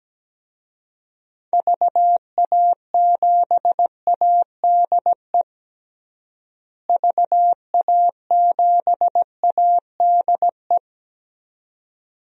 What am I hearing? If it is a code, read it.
Morse code, heard as a keyed tone, about 17 wpm, 700 Hz. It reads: VA7ADE VA7ADE
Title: DXCC-100 Call Signs - 6 Characters - Repeated 2x - 17wpm